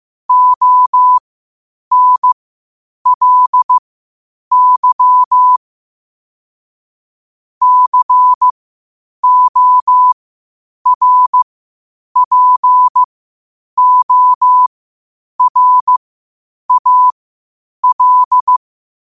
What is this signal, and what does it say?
Morse code, heard as a keyed tone, about 15 wpm, spaced as for 8 wpm, 1000 Hz. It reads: ONLY CORPORAL
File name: qotdfiles / 8wpm